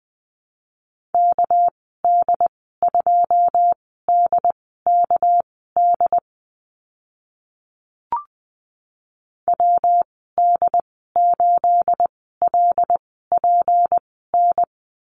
Morse code, heard as a keyed tone, about 20 wpm, spaced as for 14 wpm, 700 Hz. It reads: KD2DKD WD8LPN
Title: 219 Lesson 35g - Number 0 - Call Signs 20wpm